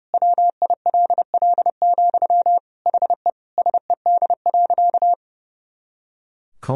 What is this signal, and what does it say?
Morse code, heard as a keyed tone, about 32 wpm, 700 Hz. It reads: WILL, HE SED.
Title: Sentences from Top 200 Words and Abbreviations 30wpm